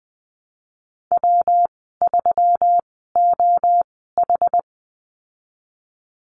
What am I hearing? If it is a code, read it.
Morse code, heard as a keyed tone, about 20 wpm, spaced as for 15 wpm, 700 Hz. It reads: W3OH
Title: 122 Lesson 20g - Number 3 - Call Signs 20wpm